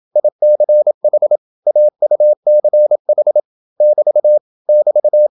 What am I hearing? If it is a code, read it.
Morse code, heard as a keyed tone, about 27 wpm, 600 Hz. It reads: ICH AUCH <BT> <BT>